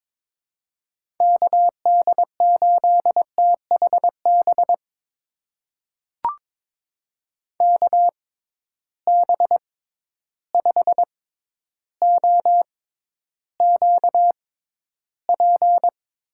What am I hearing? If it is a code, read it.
Morse code, heard as a keyed tone, about 22 wpm, spaced as for 11 wpm, 700 Hz. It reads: KD8THB KB5OQP
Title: USA Call Signs - 6 Characters - 6x Character Spacing - 22wpm